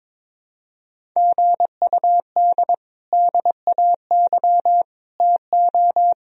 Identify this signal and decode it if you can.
Morse code, heard as a keyed tone, about 22 wpm, 700 Hz. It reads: GUD DAY TO
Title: Sentences from Top 200 Words and Abbreviations 22wpm